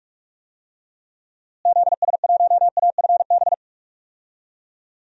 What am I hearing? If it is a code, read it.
Morse code, heard as a keyed tone, about 41 wpm, 700 Hz. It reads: ZS1AFB
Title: DXCC-100 Call Signs - 6 Characters - Repeated 3x - 45wpm